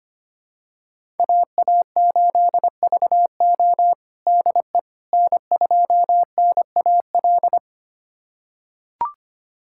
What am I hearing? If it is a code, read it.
Morse code, heard as a keyed tone, about 25 wpm, 700 Hz. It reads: AA8VO DE N2NAL
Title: USA Callsign DE Callsign - 5 Characters - 25wpm